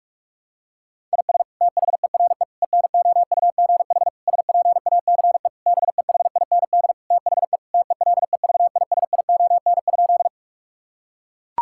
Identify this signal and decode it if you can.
Morse code, heard as a keyed tone, about 41 wpm, 700 Hz. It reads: IS THERE ENOUGH SPACE BEHIND THE TELEVISION?
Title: Sentences from Top 1000 Words - Review Part 5 - 45wpm